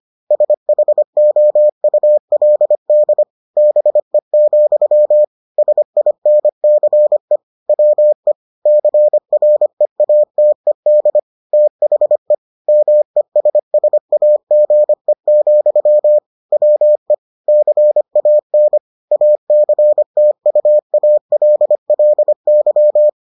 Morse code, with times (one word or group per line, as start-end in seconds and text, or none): 0.3-3.2 SHOULD
3.6-5.2 BE,
5.6-7.4 SINCE
7.7-8.3 WE
8.7-11.2 CREATED
11.5-12.3 THE
12.7-16.2 MESSAGE,
16.5-17.1 WE
17.5-18.8 CAN
19.1-23.1 ACTUALLY